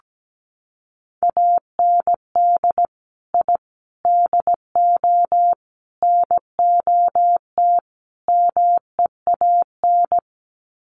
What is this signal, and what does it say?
Morse code, heard as a keyed tone, about 17 wpm, 700 Hz. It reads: AND I DO NOT MEAN
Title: Sets of 5 Words Encore2 17wpm